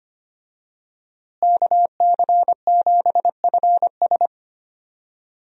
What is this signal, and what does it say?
Morse code, heard as a keyed tone, about 25 wpm, 700 Hz. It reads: KC7FS
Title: USA Call Signs - 5 Characters - Repeated 3x - 25wpm